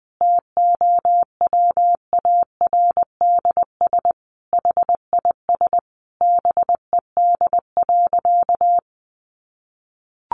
Morse code, heard as a keyed tone, about 20 wpm, 700 Hz. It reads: TOWARDS HIS BED.
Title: Sentences from Top 1100 Words 20wpm